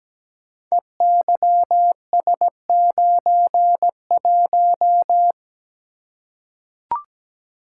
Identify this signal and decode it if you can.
Morse code, heard as a keyed tone, about 17 wpm, 700 Hz. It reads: EYS91
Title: Code Groups - 5 Letter:Numbers - 17wpm